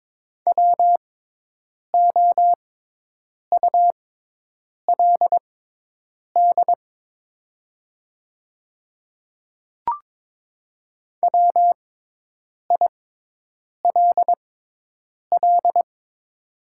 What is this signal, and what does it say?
Morse code, heard as a keyed tone, about 22 wpm, spaced as for 8 wpm, 700 Hz. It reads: WOULD WILL
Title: 6x Character Spacing - Top 100 Words - 22wpm